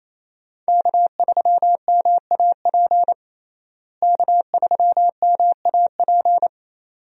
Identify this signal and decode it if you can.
Morse code, heard as a keyed tone, about 28 wpm, 700 Hz. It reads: K3MAP K3MAP
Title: USA Call Signs - 5 Characters - Repeated 2x - 28wpm